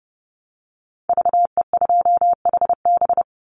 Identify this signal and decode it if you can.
Morse code, heard as a keyed tone, about 32 wpm, 700 Hz. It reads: VE2HB
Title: Call Signs - 30wpm